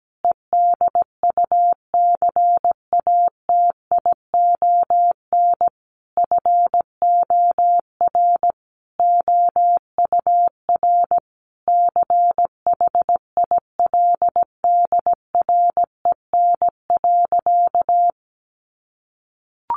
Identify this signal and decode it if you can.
Morse code, heard as a keyed tone, about 17 wpm, 700 Hz. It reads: EDUCATION FOR OUR CHILDREN.
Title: Sentences from Top 1400 Words 17wpm